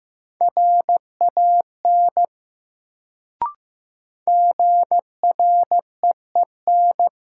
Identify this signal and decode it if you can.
Morse code, heard as a keyed tone, about 15 wpm, 700 Hz. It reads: RAN GREEN
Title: Sentences from Top 500 Words - New Word Drill - 15wpm